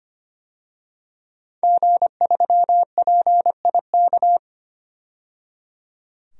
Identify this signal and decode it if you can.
Morse code, heard as a keyed tone, about 25 wpm, 700 Hz. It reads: G3PIK
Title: DXCC-100 Call Signs - 5 Characters - 25wpm